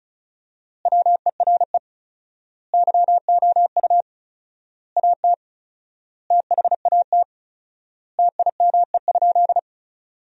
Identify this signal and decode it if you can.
Morse code, heard as a keyed tone, about 36 wpm, 700 Hz. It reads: WERE YOU AT THAT TIME?
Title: 4x Word-Spacing - Sentences from Top 200 Words - 35wpm